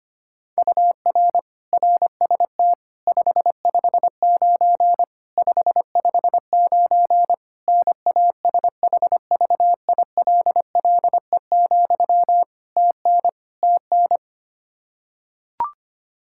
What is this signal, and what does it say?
Morse code, heard as a keyed tone, about 25 wpm, 700 Hz. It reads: UR RST 559 559 NASHVILLE, TN TN
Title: Practice QSO - Exchange 1 - Beginning - 25wpm